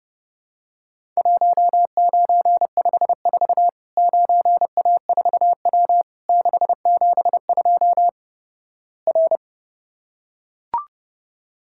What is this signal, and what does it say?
Morse code, heard as a keyed tone, about 32 wpm, 700 Hz. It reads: 1954 9A4W 672 R
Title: Worked All Europe DX Contest - QTC Exchange - 30wpm